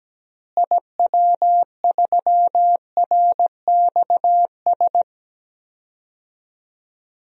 Morse code, heard as a keyed tone, about 17 wpm, 700 Hz. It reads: IW3RXS
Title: DXCC-100 Call Signs - 6 Characters - Repeated 3x - 17wpm